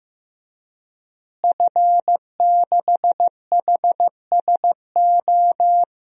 Morse code, heard as a keyed tone, about 15 wpm, 700 Hz. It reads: F6HSO
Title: DXCC-100 Call Signs - 5 Characters - Repeated 3x - 15wpm